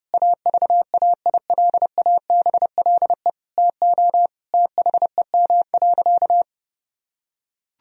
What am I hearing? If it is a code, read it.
Morse code, heard as a keyed tone, about 30 wpm, 700 Hz. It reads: AVAILABLE TO THEM.
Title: Sentences from Top 1000 Words - Review Part 3 - 30wpm